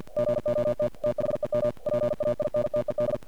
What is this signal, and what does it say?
Morse code, heard as a keyed tone, about 48 wpm, 590 Hz. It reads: GOT THEM PAINTED
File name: archived / CRSnet